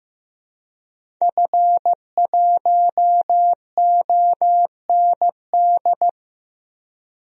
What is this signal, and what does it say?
Morse code, heard as a keyed tone, about 15 wpm, 700 Hz. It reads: F1OND
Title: DXCC-100 Call Signs - 5 Characters - Repeated 2x - 15wpm